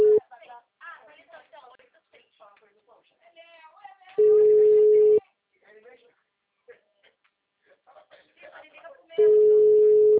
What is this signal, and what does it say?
Ringback tone (Europe, 425 Hz), with speech.